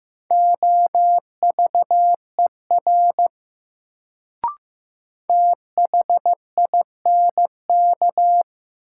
Morse code, heard as a keyed tone, about 15 wpm, 700 Hz. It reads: OVER THINK
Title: Sets of 2 Words - New Word Drill - 15wpm